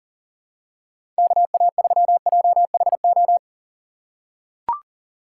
Morse code, heard as a keyed tone, about 40 wpm, 700 Hz. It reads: KA3JHO
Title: USA Call Signs - 6 Characters - 6x Character Spacing - 40wpm